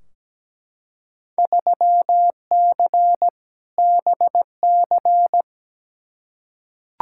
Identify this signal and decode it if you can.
Morse code, heard as a keyed tone, about 17 wpm, 700 Hz. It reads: 3C BC
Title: Field Day Exchanges Encore - 17wpm